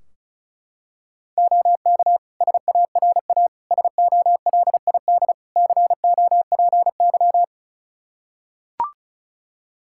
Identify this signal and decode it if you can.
Morse code, heard as a keyed tone, about 36 wpm, 700 Hz. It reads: OK SARA SOLID COPY
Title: Practice QSO - Exchange 3 - More Info - 35wpm